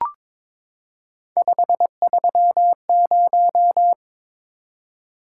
Morse code, heard as a keyed tone, about 22 wpm, 700 Hz. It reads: 530
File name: practice / 22wpm / Three Letter-Number Combinations - 22wpm